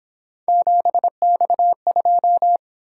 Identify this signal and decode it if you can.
Morse code, heard as a keyed tone, about 26 wpm, 700 Hz. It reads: 7X2
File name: Speed-Racing - Three Letter-Number Combinations - 22wpm